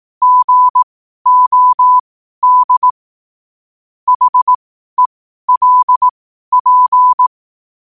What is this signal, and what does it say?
Morse code, heard as a keyed tone, about 18 wpm, spaced as for 12 wpm, 1000 Hz. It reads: GOD HELP
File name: qotdfiles / 12wpm